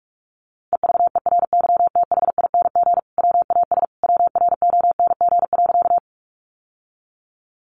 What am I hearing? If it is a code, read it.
Morse code, heard as a keyed tone, about 44 wpm, 700 Hz. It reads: EVERYTHING WAS WRONG.
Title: Sentences from Top 900 Words 45wpm